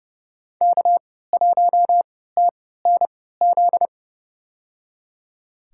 Morse code, heard as a keyed tone, about 30 wpm, spaced as for 17 wpm, 700 Hz. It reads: K1TNZ
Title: USA Call Signs - 5 Characters - 3x Character Spacing - 30wpm